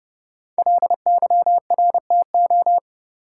Morse code, heard as a keyed tone, about 30 wpm, 700 Hz. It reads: LYRTO
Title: Code Groups - 5 Letters - 30wpm